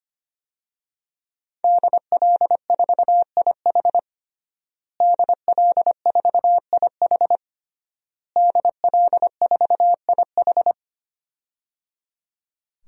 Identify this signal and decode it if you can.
Morse code, heard as a keyed tone, about 25 wpm, 700 Hz. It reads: DL4IH DL4IH DL4IH
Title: DXCC-100 Call Signs - 5 Characters - Repeated 3x - 25wpm